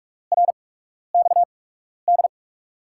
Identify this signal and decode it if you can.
Morse code, heard as a keyed tone, about 41 wpm, spaced as for 12 wpm, 700 Hz. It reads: RXD
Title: USA Call Signs - 5 Characters - 8x Character Spacing - 45wpm